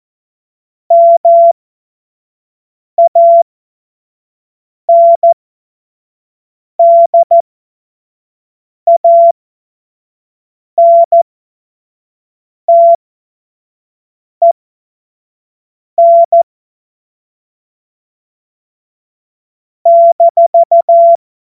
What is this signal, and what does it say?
Morse code, heard as a keyed tone, about 14 wpm, spaced as for 5 wpm, 680 Hz. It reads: MANDANTEN -
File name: Morsekurs / Witze / Witz 21. Juni 2022 25 F70